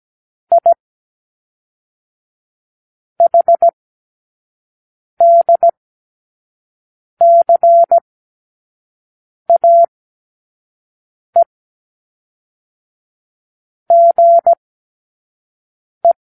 Morse code, heard as a keyed tone, about 17 wpm, 680 Hz. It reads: I H D C A E G E
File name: k7qo / Lesson18(A-L)